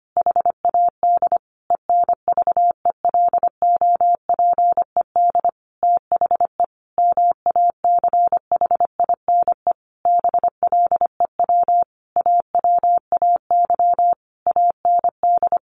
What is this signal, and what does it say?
Morse code, heard as a keyed tone, about 25 wpm, 700 Hz. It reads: HAD ENVELOPED THE MACHINE BLEW AWAY AND
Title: A Sound Of Thunder 25wpm